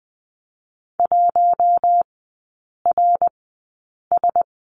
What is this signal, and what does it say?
Morse code, heard as a keyed tone, about 20 wpm, spaced as for 8 wpm, 700 Hz. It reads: 1RS
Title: 053 Lesson 10e - Letter r - Three Characters 20wpm